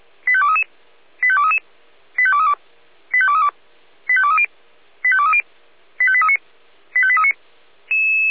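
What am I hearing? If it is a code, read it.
Five-tone selective call (ZVEI), heard as quick runs of five tones, tones of about 70 ms. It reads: 85233 85233 85232 85232 85230 85230 85839 85839
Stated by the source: ZVEI five-tone selective call received via SDR.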